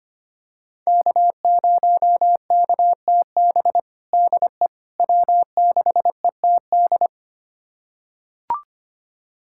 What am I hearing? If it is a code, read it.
Morse code, heard as a keyed tone, about 25 wpm, 700 Hz. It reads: K0KTB DE W6ETD
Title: USA Callsign DE Callsign - 5 Characters - 25wpm